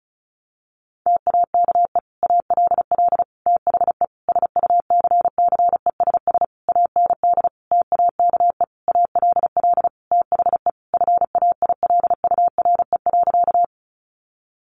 Morse code, heard as a keyed tone, about 34 wpm, 700 Hz. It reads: TAKE ALL THE SUCCESS AND TAKE ALL THE FAILURE.
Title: Sentences from Top 900 Words 35wpm